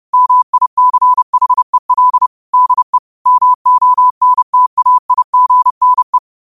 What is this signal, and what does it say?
Morse code, heard as a keyed tone, about 30 wpm, 1000 Hz. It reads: MICHEL DE MONTAIGNE
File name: qotdfiles / 30wpm